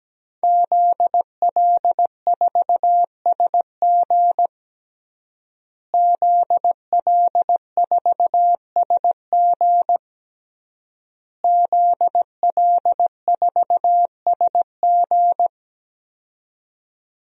Morse code, heard as a keyed tone, about 17 wpm, 700 Hz. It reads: ZL4SG ZL4SG ZL4SG
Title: DXCC-100 Call Signs - 5 Characters - Repeated 3x - 17wpm